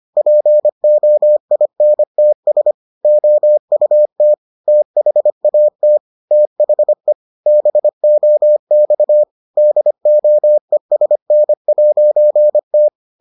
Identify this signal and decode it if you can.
Morse code, heard as a keyed tone, about 25 wpm, 600 Hz. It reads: POINTS OUT THAT THE BOX DOESN'T